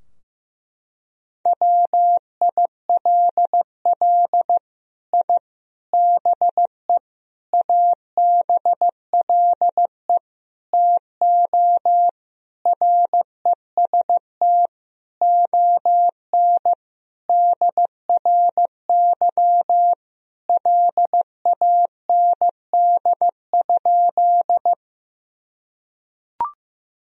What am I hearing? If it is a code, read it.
Morse code, heard as a keyed tone, about 15 wpm, 700 Hz. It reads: WILL I BE ABLE TO REST ON DRY LAND?
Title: Sentences from Top 1000 Words - Review Part 3 - 15wpm